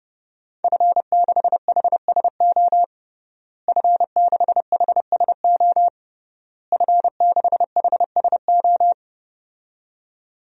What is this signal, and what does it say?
Morse code, heard as a keyed tone, about 30 wpm, 700 Hz. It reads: F6HSO F6HSO F6HSO
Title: DXCC-100 Call Signs - 5 Characters - Repeated 3x - 30wpm